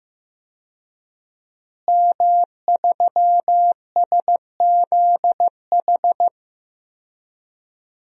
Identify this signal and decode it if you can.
Morse code, heard as a keyed tone, about 15 wpm, 700 Hz. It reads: M3SZH
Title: DXCC-100 Call Signs - 5 Characters - Repeated 3x - 15wpm